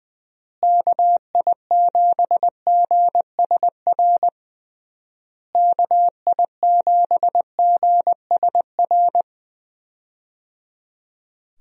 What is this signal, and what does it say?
Morse code, heard as a keyed tone, about 20 wpm, 700 Hz. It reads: KI7GSR KI7GSR
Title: USA Call Signs - 6 Characters - Repeated 3x - 20wpm